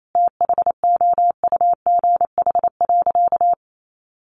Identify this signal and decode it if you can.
Morse code, heard as a keyed tone, about 28 wpm, 700 Hz. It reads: THOUGH.
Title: Sentences from Top 1400 Words 28wpm